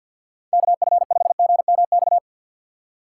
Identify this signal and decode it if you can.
Morse code, heard as a keyed tone, about 49 wpm, 700 Hz. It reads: KF5GMX
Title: USA Call Signs - 6 Characters - 4x Character Spacing - 50wpm